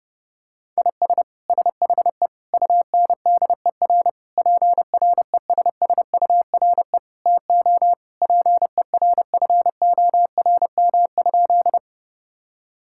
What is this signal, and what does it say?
Morse code, heard as a keyed tone, about 32 wpm, 700 Hz. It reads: IS SHE UNDER PRESSURE TO PERFORM?
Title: Sentences from Top 1000 Words - Review Part 6 - 30wpm